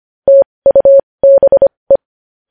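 Morse code, heard as a keyed tone, about 25 wpm, spaced as for 20 wpm, 550 Hz. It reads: TUBE
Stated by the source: Morse code news headlines practice at 20 wpm.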